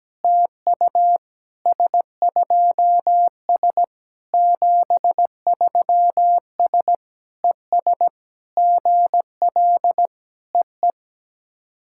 Morse code, heard as a keyed tone, about 17 wpm, 690 Hz. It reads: TU S2S 73S ES GL EE
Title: SOTA QSO Practice 17wpm